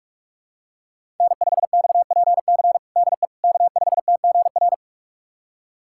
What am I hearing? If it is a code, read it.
Morse code, heard as a keyed tone, about 46 wpm, 700 Hz. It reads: N5XPC DE K5TGR
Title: USA Callsign DE Callsign - 5 Characters - 45wpm